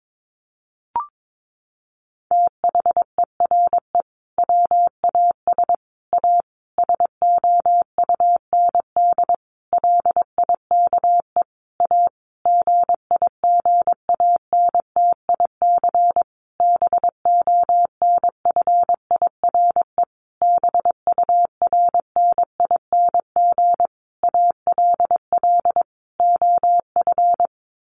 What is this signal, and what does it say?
Morse code, heard as a keyed tone, about 22 wpm, 700 Hz. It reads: THERE WAS A SOUND LIKE A GIGANTIC BONFIRE BURNING ALL OF
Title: A Sound Of Thunder 22wpm